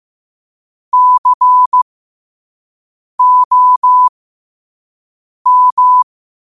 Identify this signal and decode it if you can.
Morse code, heard as a keyed tone, about 15 wpm, spaced as for 5 wpm, 1000 Hz. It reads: COM
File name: qotdfiles / 5wpm